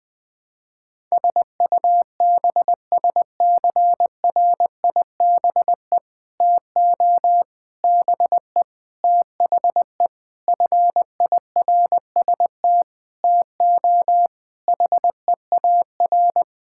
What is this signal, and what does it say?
Morse code, heard as a keyed tone, about 20 wpm, 700 Hz. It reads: SUBSCRIBE TO BE THE FIRST TO HEAR
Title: Sentences with 6-Word Phrases - 20wpm